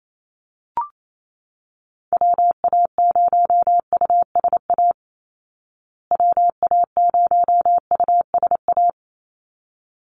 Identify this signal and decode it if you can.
Morse code, heard as a keyed tone, about 28 wpm, 700 Hz. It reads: WA0USA WA0USA
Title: Call Signs Repeated 3x - 28wpm